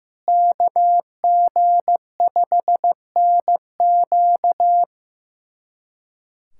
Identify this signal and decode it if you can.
Morse code, heard as a keyed tone, about 16 wpm, 700 Hz. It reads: KG5NQ
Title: USA Callsign DE Callsign - 5 Characters - 15wpm